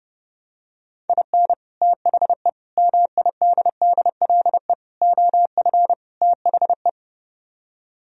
Morse code, heard as a keyed tone, about 31 wpm, 700 Hz. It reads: IN THE MIDDLE OF THE